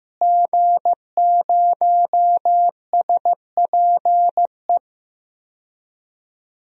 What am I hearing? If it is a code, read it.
Morse code, heard as a keyed tone, about 16 wpm, 700 Hz. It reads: G0SPE
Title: DXCC-100 Call Signs - 5 Characters - Repeated 3x - 15wpm